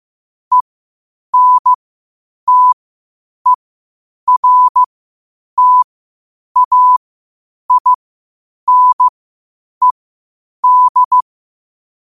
Morse code, heard as a keyed tone, about 15 wpm, spaced as for 8 wpm, 1000 Hz. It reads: ENTERTAINED
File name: qotdfiles / 8wpm